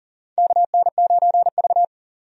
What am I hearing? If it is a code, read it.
Morse code, heard as a keyed tone, about 40 wpm, 700 Hz. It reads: KN9V
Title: USA Call Signs - 4 Characters - 40wpm